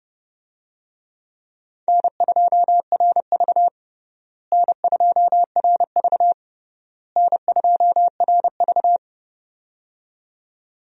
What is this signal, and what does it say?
Morse code, heard as a keyed tone, about 30 wpm, 700 Hz. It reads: N2RV N2RV N2RV
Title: USA Call Signs - 4 Characters - Repeated 3x - 30wpm